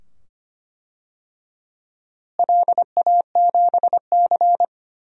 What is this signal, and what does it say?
Morse code, heard as a keyed tone, about 25 wpm, 700 Hz. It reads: LA7C
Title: DXCC-100 Call Signs - 4 Characters - Repeated 2x - 25wpm